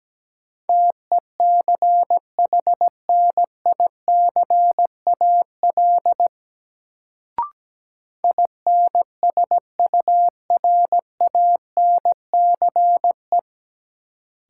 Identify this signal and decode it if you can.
Morse code, heard as a keyed tone, about 17 wpm, 700 Hz. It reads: TECHNICAL INSURANCE
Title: Sentences from Top 1100 Words - New Word Drill Alt - 17wpm